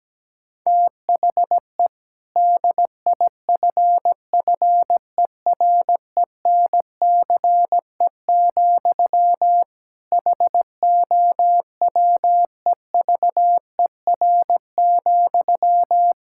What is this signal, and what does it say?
Morse code, heard as a keyed tone, about 17 wpm, 700 Hz. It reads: THE DIFFERENCE, HOWEVER,
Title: Sentences from Top 1000 Words - Review Part 5 - 17wpm